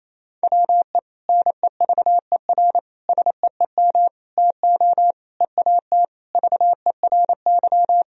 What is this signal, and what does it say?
Morse code, heard as a keyed tone, about 28 wpm, 700 Hz. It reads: WE NEVER SEEM TO EAT VERY
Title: Sentences from Top 1000 Words - Review Part 1 - 28wpm